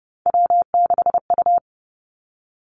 Morse code, heard as a keyed tone, about 30 wpm, 700 Hz. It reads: W6U
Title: Three Challenging Letter-Number Combinations 30wpm